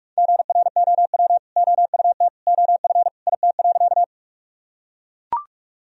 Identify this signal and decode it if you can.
Morse code, heard as a keyed tone, about 44 wpm, 700 Hz. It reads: GROW OUT OF IT.
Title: Sentences from Top 1000 Words - Review Part 1 - 45wpm